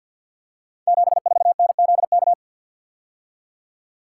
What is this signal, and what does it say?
Morse code, heard as a keyed tone, about 53 wpm, 700 Hz. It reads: 74NZK
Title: Code Groups - 5 Letter:Numbers - 50wpm